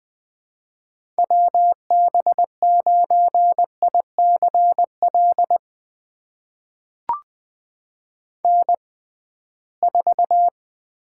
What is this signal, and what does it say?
Morse code, heard as a keyed tone, about 20 wpm, spaced as for 15 wpm, 700 Hz. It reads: WB9ICL N4
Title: USA Call Signs - 6x Character Spacing - 20wpm